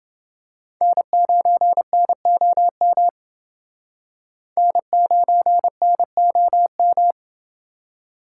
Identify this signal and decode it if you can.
Morse code, heard as a keyed tone, about 28 wpm, 700 Hz. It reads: N9NOM N9NOM